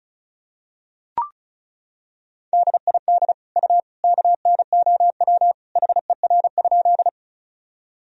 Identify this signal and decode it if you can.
Morse code, heard as a keyed tone, about 36 wpm, 700 Hz. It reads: DID U KNOW HER?
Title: Sentences from Top 100 Words and Abbreviations 35wpm